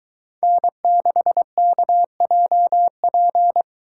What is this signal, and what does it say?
Morse code, heard as a keyed tone, about 22 wpm, 700 Hz. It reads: N6KJP